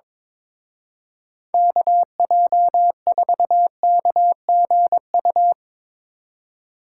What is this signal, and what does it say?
Morse code, heard as a keyed tone, about 24 wpm, 690 Hz. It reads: KJ4KGU
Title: Speed Racing - USA Call Signs 22wpm